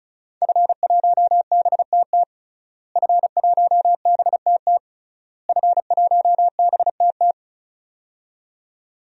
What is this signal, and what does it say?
Morse code, heard as a keyed tone, about 36 wpm, 700 Hz. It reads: F1BTT F1BTT F1BTT
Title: DXCC-100 Call Signs - 5 Characters - Repeated 3x - 35wpm